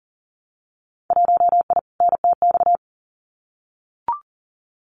Morse code, heard as a keyed tone, about 41 wpm, 700 Hz. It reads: 1I NTX
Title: Winter Field Day Exchanges Encore 40wpm